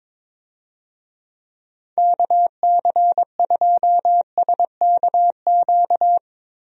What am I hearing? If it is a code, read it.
Morse code, heard as a keyed tone, about 22 wpm, 700 Hz. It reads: KC2SKQ
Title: USA Call Signs - 6 Characters - 22wpm